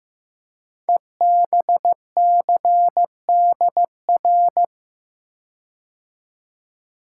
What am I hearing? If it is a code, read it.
Morse code, heard as a keyed tone, about 15 wpm, 700 Hz. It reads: EBCDR
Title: Code Groups - 5 Letters - 15wpm